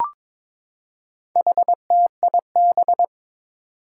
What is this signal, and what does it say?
Morse code, heard as a keyed tone, about 22 wpm, 700 Hz. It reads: HTIB
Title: Four Challenging Letter-Number Combinations 22wpm